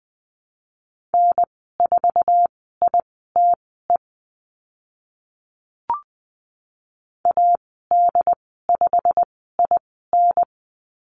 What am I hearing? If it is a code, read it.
Morse code, heard as a keyed tone, about 20 wpm, spaced as for 15 wpm, 700 Hz. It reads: N4ITE AD5IN
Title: 088 Lesson 15g - Number 5 - Call Signs 20wpm